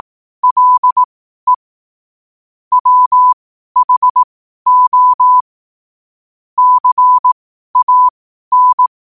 Morse code, heard as a keyed tone, about 18 wpm, spaced as for 12 wpm, 1000 Hz. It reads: LE WHO CAN
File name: qotdfiles / 12wpm